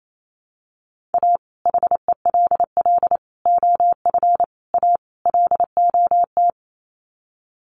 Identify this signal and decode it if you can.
Morse code, heard as a keyed tone, about 28 wpm, 700 Hz. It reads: A HELL OF A LOT
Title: Sets of 5 Words Encore 28wpm